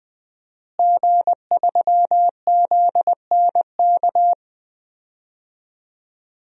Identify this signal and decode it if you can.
Morse code, heard as a keyed tone, about 21 wpm, 700 Hz. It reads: G3ZNK
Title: DXCC-100 Call Signs - 5 Characters - Repeated 3x - 20wpm